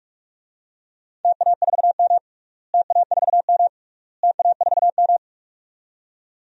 Morse code, heard as a keyed tone, about 41 wpm, 690 Hz. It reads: TA4M TA4M TA4M
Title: DXCC-100 Call Signs - 4 Characters - Repeated 3x - 45wpm